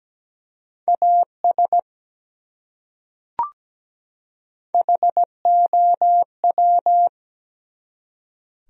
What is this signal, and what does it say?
Morse code, heard as a keyed tone, about 17 wpm, 700 Hz. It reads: AS HOW
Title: Top 100 Words 17wpm